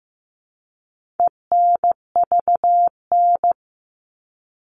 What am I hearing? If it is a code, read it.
Morse code, heard as a keyed tone, about 15 wpm, 700 Hz. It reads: ENVN
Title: All Cut Numbers - 4-Digit 15wpm